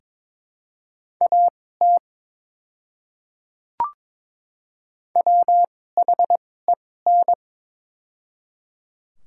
Morse code, heard as a keyed tone, about 22 wpm, spaced as for 16 wpm, 700 Hz. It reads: AT WHEN
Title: 2x Character Spacing - Top 100 Words - 22wpm